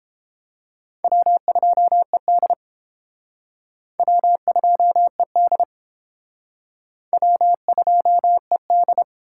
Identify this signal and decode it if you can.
Morse code, heard as a keyed tone, about 29 wpm, 700 Hz. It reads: W2ED W2ED W2ED